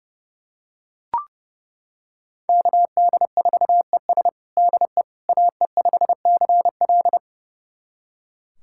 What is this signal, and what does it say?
Morse code, heard as a keyed tone, about 30 wpm, 700 Hz. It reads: KD4ES DE AE5CL
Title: USA Callsign DE Callsign - 5 Characters - 30wpm